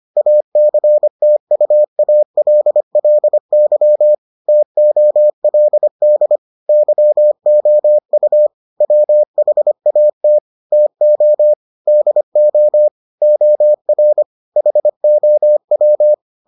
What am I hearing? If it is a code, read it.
Morse code, heard as a keyed tone, about 25 wpm, 600 Hz. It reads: ACTUALLY TOLD YOU WHAT TO DO OR HOW